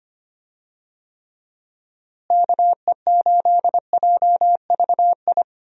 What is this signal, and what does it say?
Morse code, heard as a keyed tone, about 25 wpm, 700 Hz. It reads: KE8JVI
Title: USA Call Signs - 6 Characters - Repeated 2x - 25wpm